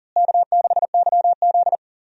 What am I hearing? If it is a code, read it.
Morse code, heard as a keyed tone, about 40 wpm, 700 Hz. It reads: K6YZ
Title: USA Call Signs - 4 Characters - Repeated 3x - 40wpm